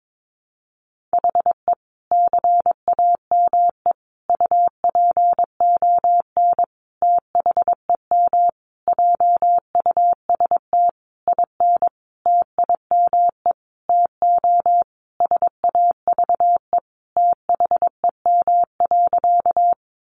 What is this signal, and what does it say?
Morse code, heard as a keyed tone, about 22 wpm, 700 Hz. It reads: HE CAME UPON THEM JUST IN TIME TO SAVE THEM.
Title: Sentences from Top 1100 Words 22wpm